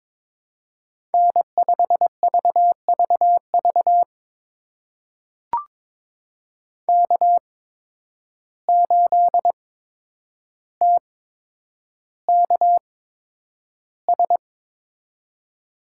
Morse code, heard as a keyed tone, about 22 wpm, spaced as for 9 wpm, 700 Hz. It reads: N5VVV K8TKS
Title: USA Call Signs - 5 Characters - 8x Character Spacing - 22wpm